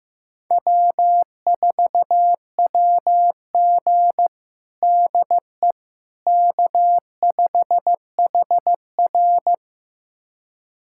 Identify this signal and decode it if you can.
Morse code, heard as a keyed tone, about 15 wpm, 700 Hz. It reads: W4WG DE K5HR
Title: USA Callsign DE Callsign - 4 Characters - 15wpm